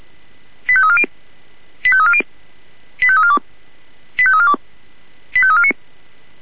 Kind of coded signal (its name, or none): five-tone selective call